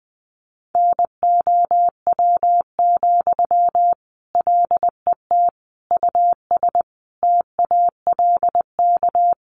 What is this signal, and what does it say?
Morse code, heard as a keyed tone, about 20 wpm, 700 Hz. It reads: NOW, LET US TALK
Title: Sentences from Top 800 Words 20wpm